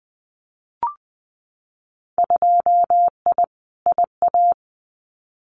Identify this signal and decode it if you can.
Morse code, heard as a keyed tone, about 20 wpm, 700 Hz. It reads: 2I IA
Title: Winter Field Day Exchanges Encore 20wpm